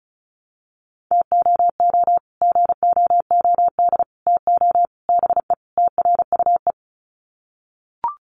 Morse code, heard as a keyed tone, about 34 wpm, 700 Hz. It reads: TOO GOOD TO BE TRUE
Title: Sets of 5 Words Encore 35wpm